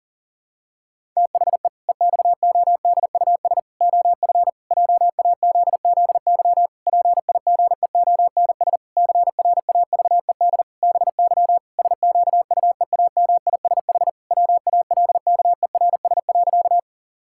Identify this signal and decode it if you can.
Morse code, heard as a keyed tone, about 40 wpm, 700 Hz. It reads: THE EXODUS OF JAZZY PIGEONS CRAVED BY SQUEAMISH WALKERS.
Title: Pangram Challenge v2 - 40wpm